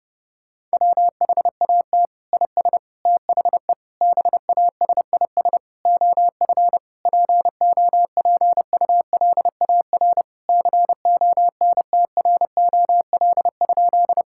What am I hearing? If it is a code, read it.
Morse code, heard as a keyed tone, about 30 wpm, 700 Hz. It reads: WHAT IS THE BASIS OF POPULAR CONTROL?
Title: Sentences from Top 1000 Words - Review Part 5 - 30wpm